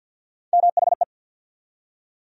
Morse code, heard as a keyed tone, about 52 wpm, 700 Hz. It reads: MHE
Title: USA Call Sign Postfixes - 3 Letters 50wpm